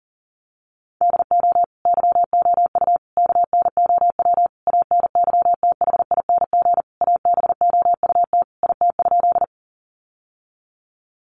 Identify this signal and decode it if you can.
Morse code, heard as a keyed tone, about 40 wpm, 700 Hz. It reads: DO YOU KNOW ANYTHING ABOUT IT?
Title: Sentences from Top 1000 Words 40wpm